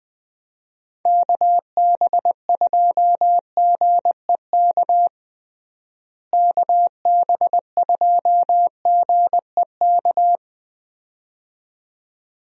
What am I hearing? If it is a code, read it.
Morse code, heard as a keyed tone, about 21 wpm, 700 Hz. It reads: KB2GEK KB2GEK
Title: USA Call Signs - 6 Characters - Repeated 2x - 20wpm